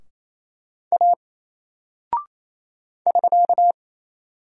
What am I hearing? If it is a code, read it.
Morse code, heard as a keyed tone, about 27 wpm, 700 Hz. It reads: A <SK>
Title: Single Character (Letter:Number:Punctuation:Prosign) 28wpm